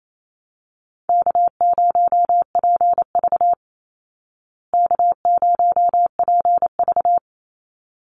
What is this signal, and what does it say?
Morse code, heard as a keyed tone, about 28 wpm, 700 Hz. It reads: K0PV K0PV
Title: Easy Call Signs Repeated 2x - 28wpm